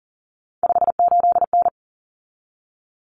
Morse code, heard as a keyed tone, about 40 wpm, 700 Hz. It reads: 58N
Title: Common Cut Numbers - 3-Digit 40wpm